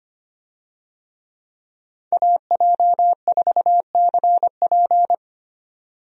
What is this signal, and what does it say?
Morse code, heard as a keyed tone, about 25 wpm, 700 Hz. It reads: AJ4CP
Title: USA Call Signs - 5 Characters - 25wpm